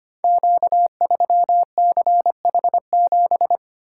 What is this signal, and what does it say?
Morse code, heard as a keyed tone, about 25 wpm, 700 Hz. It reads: Q3CH7
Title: Code Groups - 5 Letter:Numbers - 25wpm